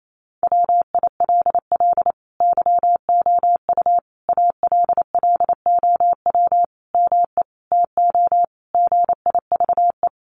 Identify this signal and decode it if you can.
Morse code, heard as a keyed tone, about 28 wpm, 700 Hz. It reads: WILL YOU ALLOW ME TO GIVE
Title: Sentences from Top 1200 Words 28wpm